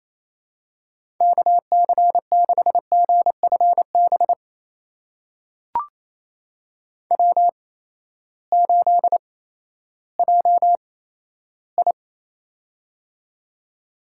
Morse code, heard as a keyed tone, about 28 wpm, spaced as for 14 wpm, 700 Hz. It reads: KC6GFB W8JI
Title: USA Call Signs - 8x Character Spacing - 28wpm